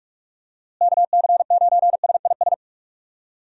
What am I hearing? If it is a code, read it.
Morse code, heard as a keyed tone, about 46 wpm, 700 Hz. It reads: KC9SIS
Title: USA Call Signs - 6 Characters - Repeated 2x - 45wpm